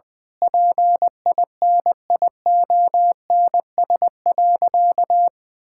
Morse code, heard as a keyed tone, about 20 wpm, 700 Hz. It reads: PINIONS.
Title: Pangram Challenge v2 - 20wpm